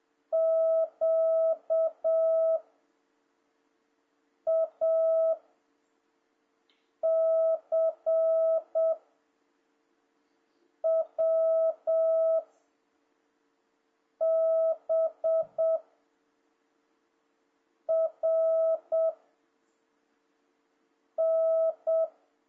Morse code, heard as a keyed tone, about 7 wpm, 650 Hz. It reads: Q A C W B R N
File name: jh8cbh / 2026cw